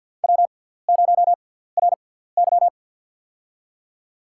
Morse code, heard as a keyed tone, about 50 wpm, spaced as for 17 wpm, 700 Hz. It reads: W0RY
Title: USA Call Signs - 4 Characters - 6x Character Spacing - 50wpm